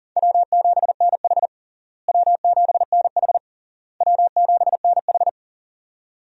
Morse code, heard as a keyed tone, about 40 wpm, 700 Hz. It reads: W7NH W7NH W7NH
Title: USA Call Signs - 4 Characters - Repeated 3x - 40wpm